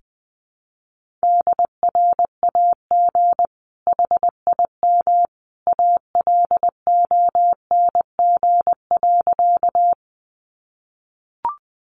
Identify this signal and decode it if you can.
Morse code, heard as a keyed tone, about 20 wpm, 700 Hz. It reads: DRAG HIM ALONG.
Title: Sentences from Top 1500 Words 20wpm